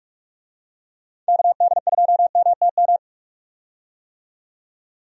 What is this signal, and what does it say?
Morse code, heard as a keyed tone, about 44 wpm, 700 Hz. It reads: KD2MTM
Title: USA Call Signs - 6 Characters - 45wpm